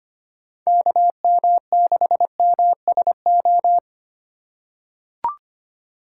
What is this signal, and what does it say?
Morse code, heard as a keyed tone, about 25 wpm, 700 Hz. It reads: KM6MSO
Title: USA Call Signs - 4x Character Spacing - 25wpm